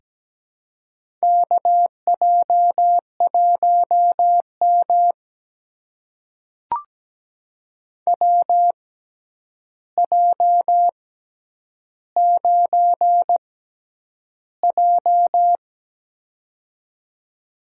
Morse code, heard as a keyed tone, about 17 wpm, spaced as for 9 wpm, 700 Hz. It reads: KJ1M WJ9J
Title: USA Call Signs - 4 Characters - 6x Character Spacing - 17wpm